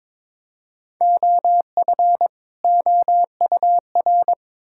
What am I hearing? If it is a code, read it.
Morse code, heard as a keyed tone, about 21 wpm, 700 Hz. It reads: OF OUR
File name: Build A Phrase - 2-Word Phrases - 22wpm